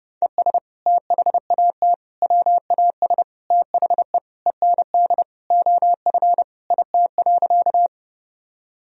Morse code, heard as a keyed tone, about 32 wpm, 700 Hz. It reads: ES THAT WAS THE END OF IT.
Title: Sentences from Top 200 Words and Abbreviations 30wpm